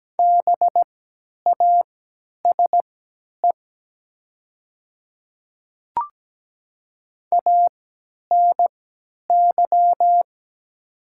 Morse code, heard as a keyed tone, about 17 wpm, spaced as for 10 wpm, 700 Hz. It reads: BASE ANY
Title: 3x Character Spacing - Top 300 Words - 17wpm